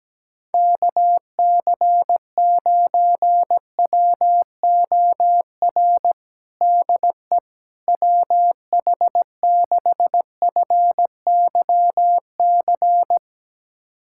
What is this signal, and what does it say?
Morse code, heard as a keyed tone, about 17 wpm, 700 Hz. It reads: KC9WOR DE WH6FYC
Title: USA Callsign DE Callsign 17wpm